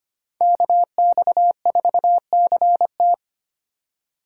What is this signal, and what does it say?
Morse code, heard as a keyed tone, about 25 wpm, 700 Hz. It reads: KX4CT
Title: USA Call Signs - 5 Characters - 6x Character Spacing - 25wpm